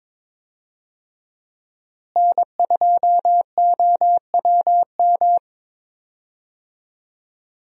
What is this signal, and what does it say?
Morse code, heard as a keyed tone, about 22 wpm, 700 Hz. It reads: N2OWM
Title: USA Call Signs - 5 Characters - Repeated 3x - 22wpm